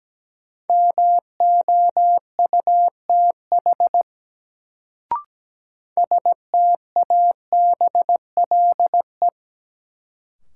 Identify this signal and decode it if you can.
Morse code, heard as a keyed tone, about 17 wpm, 700 Hz. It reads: MOUTH STABLE
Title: Sentences from Top 1500 Words - New Word Drill - 17wpm